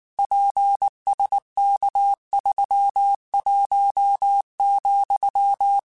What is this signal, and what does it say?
Morse code, heard as a keyed tone, about 19 wpm, 800 Hz. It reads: PSK31,
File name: QSOs / QSOs-mp3s / kc7sac@18